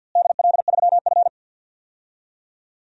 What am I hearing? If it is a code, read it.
Morse code, heard as a keyed tone, about 52 wpm, 700 Hz. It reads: DL3F
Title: DXCC-100 Call Signs - 4 Characters - Repeated 3x - 50wpm